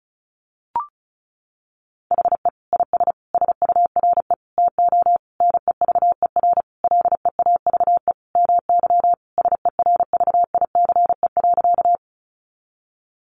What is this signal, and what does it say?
Morse code, heard as a keyed tone, about 34 wpm, 700 Hz. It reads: HE IS SURE TO NEVER LEAVE MY SERVICE.
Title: Sentences from Top 700 Words 35wpm